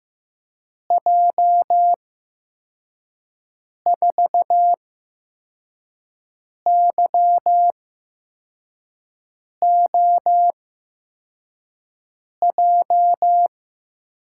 Morse code, heard as a keyed tone, about 15 wpm, spaced as for 4 wpm, 700 Hz. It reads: J4YOJ
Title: USA Call Signs - 8x Character Spacing - 15wpm